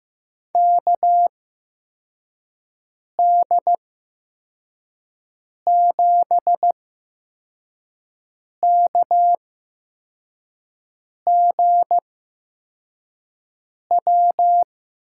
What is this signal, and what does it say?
Morse code, heard as a keyed tone, about 15 wpm, spaced as for 4 wpm, 700 Hz. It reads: KD7KGW
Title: USA Call Signs - 6 Characters - 8x Character Spacing - 15wpm